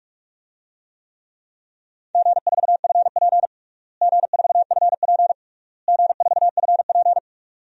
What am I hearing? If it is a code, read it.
Morse code, heard as a keyed tone, about 41 wpm, 700 Hz. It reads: G4FP G4FP G4FP
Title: DXCC-100 Call Signs - 4 Characters - Repeated 3x - 45wpm